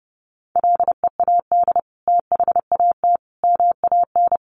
Morse code, heard as a keyed tone, about 30 wpm, 700 Hz. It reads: LEAD THAT MAN
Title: Sentences from Top 600 Words 30wpm